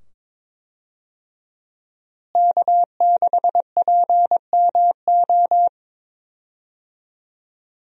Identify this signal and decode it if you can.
Morse code, heard as a keyed tone, about 22 wpm, 700 Hz. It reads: K6PMO
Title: USA Call Signs - 5 Characters - Repeated 2x - 22wpm